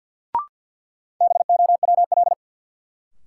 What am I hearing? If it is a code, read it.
Morse code, heard as a keyed tone, about 53 wpm, 700 Hz. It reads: BOWL
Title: Sentences from Top 1200 Words - New Word Drill - 50wpm